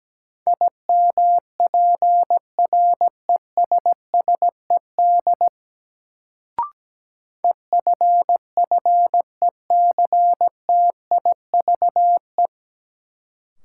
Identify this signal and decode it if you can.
Morse code, heard as a keyed tone, about 17 wpm, 700 Hz. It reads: IMPRESSED EFFECTIVE
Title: Sentences from Top 1600 Words - New Word Drill Alt - 17wpm